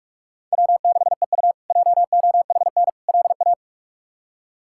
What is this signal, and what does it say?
Morse code, heard as a keyed tone, about 41 wpm, 700 Hz. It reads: W6EU JOHN LA
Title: K1USN SST QSO Practice - 45wpm